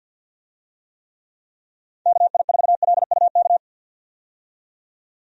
Morse code, heard as a keyed tone, about 49 wpm, 700 Hz. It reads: KI4LUK
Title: USA Call Signs - 6 Characters - 50wpm